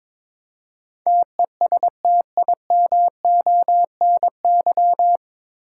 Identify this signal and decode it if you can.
Morse code, heard as a keyed tone, about 22 wpm, 700 Hz. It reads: TESTIMONY
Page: Morse Code Ninja practice 22wpm